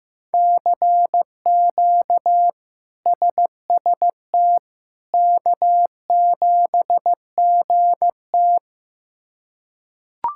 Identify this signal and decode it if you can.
Morse code, heard as a keyed tone, about 16 wpm, 700 Hz. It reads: CQ SST K7GT
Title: K1USN SST QSO Practice - 15wpm